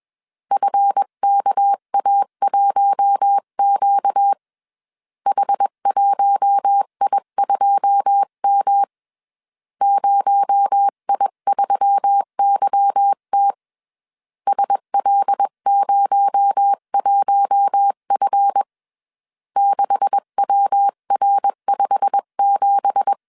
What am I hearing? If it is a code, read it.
Morse code, heard as a keyed tone, about 21 wpm, 800 Hz. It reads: FKA1Q H1I2M 0I3YT SL01F 6WR57
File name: CW / DET / 1 9 1500C S B 20wpm 2313